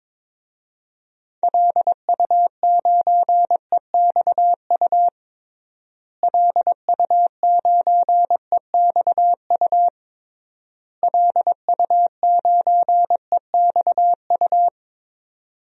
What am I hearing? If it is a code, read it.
Morse code, heard as a keyed tone, about 22 wpm, 700 Hz. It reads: LU9EXU LU9EXU LU9EXU
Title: DXCC-100 Call Signs - 6 Characters - Repeated 3x - 22wpm